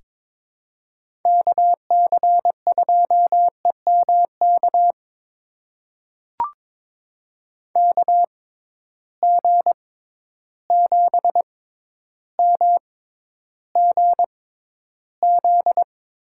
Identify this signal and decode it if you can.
Morse code, heard as a keyed tone, about 22 wpm, spaced as for 11 wpm, 700 Hz. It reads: KC2EMK KG7MGZ
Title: USA Call Signs - 6 Characters - 6x Character Spacing - 22wpm